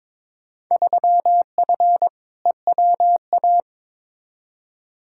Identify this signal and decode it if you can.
Morse code, heard as a keyed tone, about 22 wpm, 700 Hz. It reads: 3F EWA
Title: Field Day Exchanges Encore - 22wpm